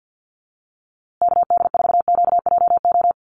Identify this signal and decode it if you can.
Morse code, heard as a keyed tone, about 50 wpm, 700 Hz. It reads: KD4QJO
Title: USA Call Signs Repeated 2x 50wpm